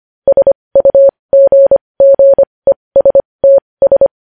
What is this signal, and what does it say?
Morse code, heard as a keyed tone, about 25 wpm, spaced as for 20 wpm, 550 Hz. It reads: SUGGESTS
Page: news-headlines-in-morse-code-at-20-wpm